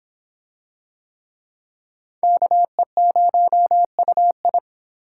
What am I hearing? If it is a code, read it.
Morse code, heard as a keyed tone, about 26 wpm, 700 Hz. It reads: KE0UI